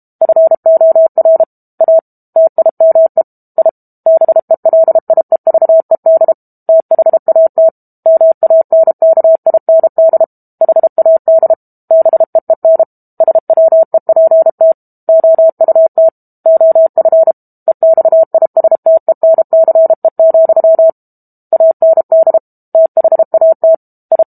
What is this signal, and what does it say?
Morse code, heard as a keyed tone, about 32 wpm, 650 Hz. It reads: FOR A TIME I BELIEVED THAT MANKIND HAD BEEN SWEPT OUT OF EXISTENCE, AND THAT I
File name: Book Two / 06-Chapter 6 - 32 WPM